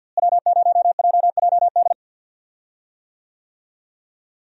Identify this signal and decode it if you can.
Morse code, heard as a keyed tone, about 52 wpm, 700 Hz. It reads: W0JJD
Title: USA Call Signs - 5 Characters - Repeated 3x - 50wpm